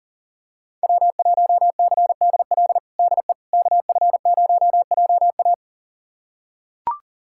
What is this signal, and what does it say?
Morse code, heard as a keyed tone, about 40 wpm, 700 Hz. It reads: W1CDL DE KF0JA
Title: USA Callsign DE Callsign - 5 Characters - 40wpm